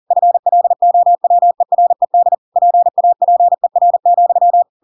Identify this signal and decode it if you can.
Morse code, heard as a keyed tone, about 40 wpm, 700 Hz. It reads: FLOWERED PAPER,
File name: Pinocchio / Pinocchio ch8 40 40 ik1hge